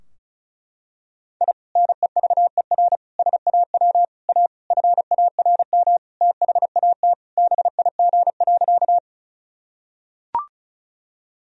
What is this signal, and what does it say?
Morse code, heard as a keyed tone, about 34 wpm, 700 Hz. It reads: I NEVER SAW A FARM THAT BIG.
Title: Sentences from Top 1000 Words - Review Part 2 - 35wpm